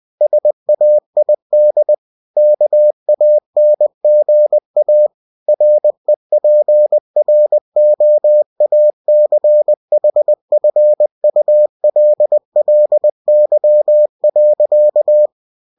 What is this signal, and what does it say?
Morse code, heard as a keyed tone, about 20 wpm, 600 Hz. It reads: SAID KANGA REPROACHFULLY.
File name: cw-books / winnie-the-pooh / winnie-20wpm-page-0108